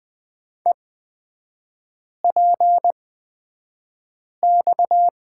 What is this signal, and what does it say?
Morse code, heard as a keyed tone, about 20 wpm, spaced as for 5 wpm, 700 Hz. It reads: EPX
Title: 255 Lesson 41e - Period - Three Characters 20wpm